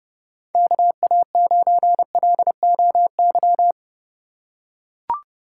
Morse code, heard as a keyed tone, about 32 wpm, 700 Hz. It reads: KA9LOY
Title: USA Call Signs - 6 Characters - 2x Character Spacing - 30wpm